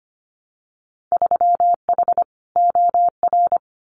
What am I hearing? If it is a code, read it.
Morse code, heard as a keyed tone, about 25 wpm, 700 Hz. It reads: 3H OR
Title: Winter Field Day Exchanges Encore 25wpm